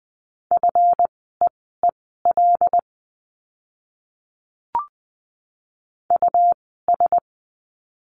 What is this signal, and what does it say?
Morse code, heard as a keyed tone, about 20 wpm, spaced as for 15 wpm, 700 Hz. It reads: FEEL US
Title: 192 Lesson 31f - Letter v - Words 20wpm